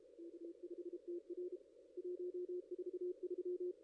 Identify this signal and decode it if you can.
Morse code, heard as a keyed tone, about 32 wpm, 360 Hz. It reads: K5TR 143
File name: ss-cw-k5tr-op-n5rz